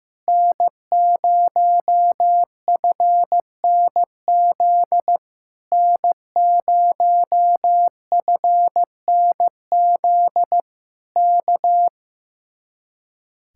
Morse code, heard as a keyed tone, about 15 wpm, 700 Hz. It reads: N0FNZ N0FNZ K
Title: Speed-Racing Common QSO Phrases Encore - 15wpm